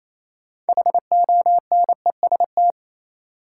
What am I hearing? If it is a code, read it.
Morse code, heard as a keyed tone, about 28 wpm, 700 Hz. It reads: HONEST
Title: Sets of 5 Words Encore - New Word Drill - 28wpm